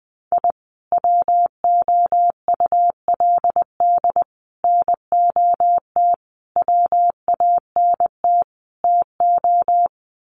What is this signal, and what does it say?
Morse code, heard as a keyed tone, about 20 wpm, 700 Hz. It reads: I WOULD NOT WANT TO
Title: Sets of 5 Words 20wpm